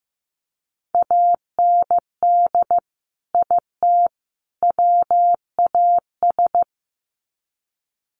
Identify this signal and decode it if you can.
Morse code, heard as a keyed tone, about 15 wpm, 700 Hz. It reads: AND IT WAS
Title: Sets of 3 Words 15wpm